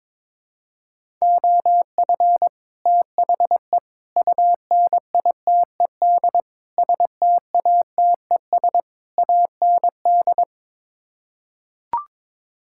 Morse code, heard as a keyed tone, about 21 wpm, 700 Hz. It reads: OF THE UNITED STATES AND
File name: practice / 22wpm / Build A Phrase - 5-Word Phrases - 22wpm